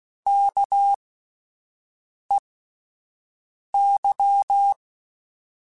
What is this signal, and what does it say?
Morse code, heard as a keyed tone, about 16 wpm, spaced as for 5 wpm, 800 Hz. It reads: KEY
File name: Alphabetical / Alphabetical mp3s / Ks @5